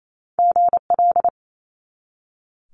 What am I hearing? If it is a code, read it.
Morse code, heard as a keyed tone, about 28 wpm, 700 Hz. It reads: GL
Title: Multi-letter Phonemes 28wpm